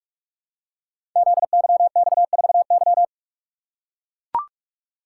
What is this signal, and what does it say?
Morse code, heard as a keyed tone, about 44 wpm, 700 Hz. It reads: ZYX4Y
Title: Code Groups - 5 Letter:Numbers - 45wpm